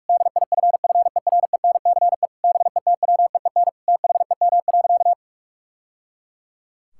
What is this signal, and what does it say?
Morse code, heard as a keyed tone, about 46 wpm, 700 Hz. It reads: DIFFERENCE BETWEEN THEM.
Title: Sentences from Top 1000 Words - Review Part 5 - 45wpm